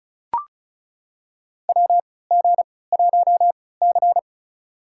Morse code, heard as a keyed tone, about 34 wpm, spaced as for 20 wpm, 700 Hz. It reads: WG1C
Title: USA Call Signs - 4 Characters - 3x Character Spacing - 35wpm